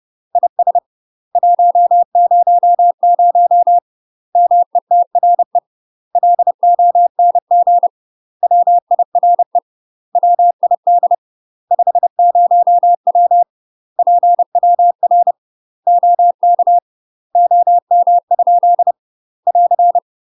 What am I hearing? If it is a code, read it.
Morse code, heard as a keyed tone, about 30 wpm, 700 Hz. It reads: IS 100 METRE LONG WIRE WID 50W PWR OK OM? <AR>
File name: UK MASTER QSOs TRAINING / 30WPM / UK MASTER QSOs 0043